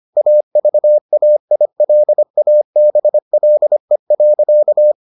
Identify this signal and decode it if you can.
Morse code, heard as a keyed tone, about 25 wpm, 600 Hz. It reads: AVAILABLE.